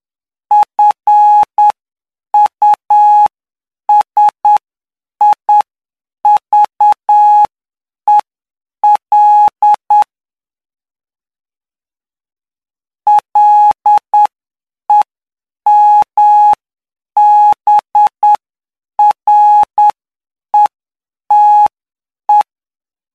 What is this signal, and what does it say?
Morse code, heard as a keyed tone, about 9 wpm, spaced as for 7 wpm, 840 Hz. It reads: FUSIVEL LEMBRETE
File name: cw / py2lcd / 08-PY2LCD-Li��o 17